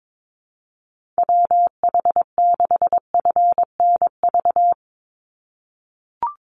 Morse code, heard as a keyed tone, about 22 wpm, 700 Hz. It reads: WH6FNV
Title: USA Call Signs 22wpm